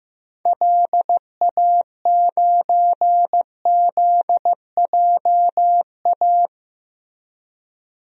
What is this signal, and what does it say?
Morse code, heard as a keyed tone, about 15 wpm, 700 Hz. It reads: LA9ZJA
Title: DXCC-100 Call Signs - 6 Characters - 15wpm